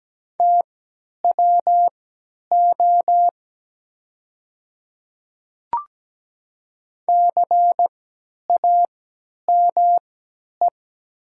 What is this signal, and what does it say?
Morse code, heard as a keyed tone, about 17 wpm, spaced as for 10 wpm, 700 Hz. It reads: TWO CAME
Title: 3x Character Spacing - Top 200 Words - 17wpm